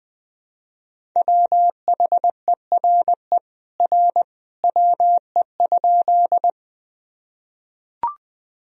Morse code, heard as a keyed tone, about 20 wpm, 700 Hz. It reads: WHERE R WE?
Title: Sentences from Top 200 Words and Abbreviations 20wpm